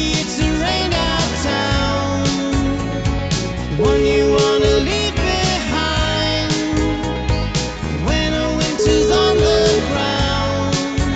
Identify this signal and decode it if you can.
Ringback tone (Europe, 425 Hz), with noise.